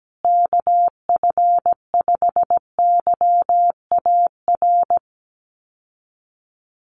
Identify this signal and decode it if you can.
Morse code, heard as a keyed tone, about 17 wpm, 700 Hz. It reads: KF5YAR
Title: USA Call Signs Repeated 3x 17wpm